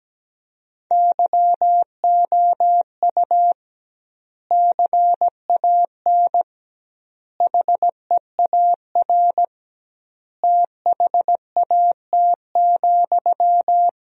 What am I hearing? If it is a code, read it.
Morse code, heard as a keyed tone, about 17 wpm, 700 Hz. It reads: YOU CAN HEAR THAT,
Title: 2x Word-Spacing - Sentences from Top 500 Words - 17wpm